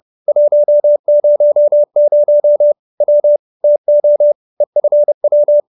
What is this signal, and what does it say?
Morse code, heard as a keyed tone, about 30 wpm, 590 Hz. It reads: 100 W TO EFW
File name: QSO321 30